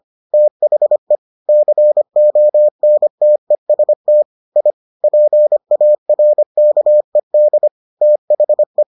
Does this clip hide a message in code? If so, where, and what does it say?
Yes: Morse code from 0.3 to 8.8 s, heard as a keyed tone, about 25 wpm, 600 Hz. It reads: THE CONTEST I PARKED THE